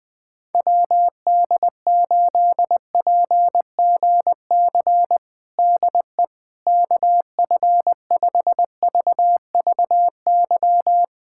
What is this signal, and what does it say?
Morse code, heard as a keyed tone, about 21 wpm, 700 Hz. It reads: WD8PGC DE KF5VVY
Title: USA Callsign DE Callsign - 6 Characters - 20wpm